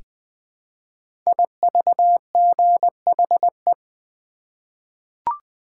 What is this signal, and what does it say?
Morse code, heard as a keyed tone, about 20 wpm, 700 Hz. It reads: IVGHE
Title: Code Groups - 5 Letters - 20wpm